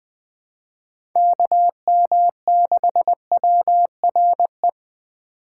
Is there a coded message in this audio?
Yes: Morse code from 1.2 to 4.7 s, heard as a keyed tone, about 20 wpm, 700 Hz. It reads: KM6WRE